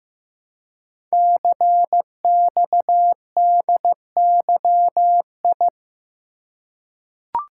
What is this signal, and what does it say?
Morse code, heard as a keyed tone, about 15 wpm, 700 Hz. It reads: CXDYI
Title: Code Groups - 5 Letters - 15wpm